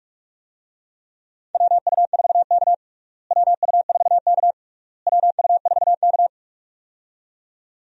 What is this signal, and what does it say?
Morse code, heard as a keyed tone, about 46 wpm, 700 Hz. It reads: WU4K WU4K WU4K
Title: USA Call Signs - 4 Characters - Repeated 3x - 45wpm